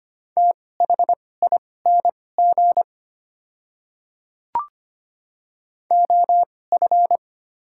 Morse code, heard as a keyed tone, about 25 wpm, spaced as for 18 wpm, 700 Hz. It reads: THING OF
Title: 2x Character Spacing - Top 100 Words - 25wpm